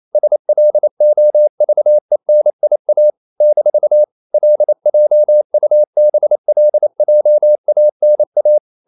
Morse code, heard as a keyed tone, about 28 wpm, 600 Hz. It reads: SLOVENIA - LJUBLJANA
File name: CW / Practice / NationalCapitals28